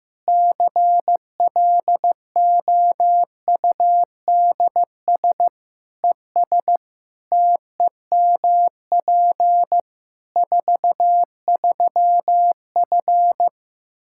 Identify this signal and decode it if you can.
Morse code, heard as a keyed tone, about 14 wpm, 700 Hz. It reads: CLOUDS ES TEMP 43F
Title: Practice QSO - Exchange 2 - Info - 15wpm